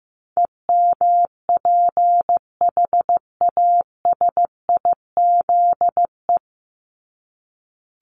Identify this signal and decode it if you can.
Morse code, heard as a keyed tone, about 15 wpm, 700 Hz. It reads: EMPHASIZE
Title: Top 1251-1500 Words 15wpm